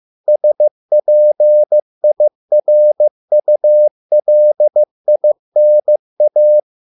Morse code, heard as a keyed tone, about 15 wpm, 600 Hz. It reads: SPIRULINA